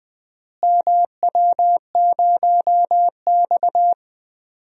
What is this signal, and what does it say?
Morse code, heard as a keyed tone, about 20 wpm, 700 Hz. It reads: MW0X
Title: DXCC-100 Call Signs - 4 Characters - Repeated 2x - 20wpm